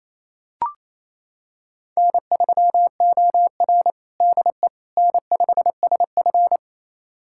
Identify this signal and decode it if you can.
Morse code, heard as a keyed tone, about 28 wpm, 700 Hz. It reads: N3OR DE N5SF
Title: USA Callsign DE Callsign - 4 Characters - 28wpm